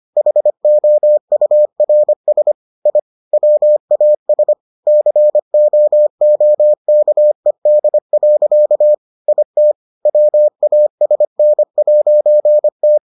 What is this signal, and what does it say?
Morse code, heard as a keyed tone, about 25 wpm, 600 Hz. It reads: HOURS I WAS COOKED. IT WASN'T